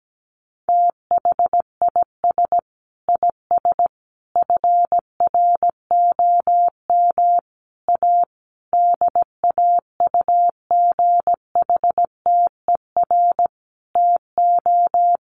Morse code, heard as a keyed tone, about 17 wpm, 700 Hz. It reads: THIS IS FROM A DAUGHTER TO
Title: Sentences from Top 1500 Words 17wpm